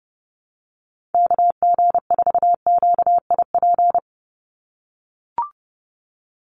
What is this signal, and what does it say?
Morse code, heard as a keyed tone, about 30 wpm, 700 Hz. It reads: KG4QIP
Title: USA Call Signs 30wpm